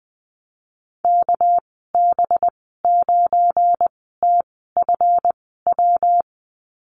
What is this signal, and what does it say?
Morse code, heard as a keyed tone, about 19 wpm, spaced as for 14 wpm, 700 Hz. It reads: KB9TFW
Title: 200 Lesson 32g - Letter k - Call Signs 20wpm